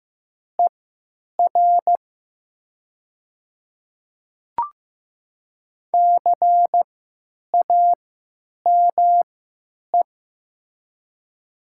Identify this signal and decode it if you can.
Morse code, heard as a keyed tone, about 15 wpm, spaced as for 9 wpm, 700 Hz. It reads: ER CAME
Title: 3x Character Spacing - Top 300 Words - 15wpm